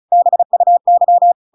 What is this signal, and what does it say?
Morse code, heard as a keyed tone, about 34 wpm, 700 Hz. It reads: BUY
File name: Pinocchio / Pinocchio ch8 35 35 ik1hge